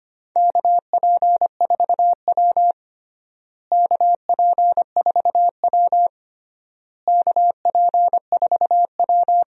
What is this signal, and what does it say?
Morse code, heard as a keyed tone, about 25 wpm, 700 Hz. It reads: KP4W KP4W KP4W
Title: DXCC-100 Call Signs - 4 Characters - Repeated 3x - 25wpm